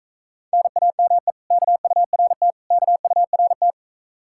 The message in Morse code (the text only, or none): NAME KURT KURT